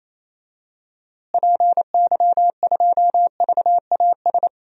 Morse code, heard as a keyed tone, about 28 wpm, 700 Hz. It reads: PY2VAS
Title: DXCC-100 Call Signs - 6 Characters - Repeated 2x - 28wpm